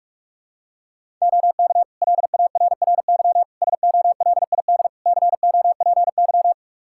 Morse code, heard as a keyed tone, about 42 wpm, 700 Hz. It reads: OK LARRY SOLID COPY
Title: Practice QSO - Exchange 3 - More Info - 45wpm